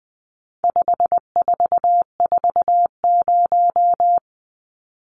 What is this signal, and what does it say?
Morse code, heard as a keyed tone, about 20 wpm, 700 Hz. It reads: 5440
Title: Numbers - 4-Digit 20wpm